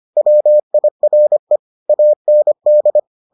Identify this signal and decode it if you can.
Morse code, heard as a keyed tone, about 25 wpm, 600 Hz. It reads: WIRE AND